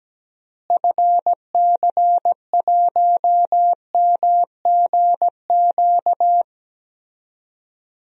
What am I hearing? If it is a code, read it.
Morse code, heard as a keyed tone, about 17 wpm, 700 Hz. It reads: FC1MGQ
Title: DXCC-100 Call Signs - 6 Characters - Repeated 2x - 17wpm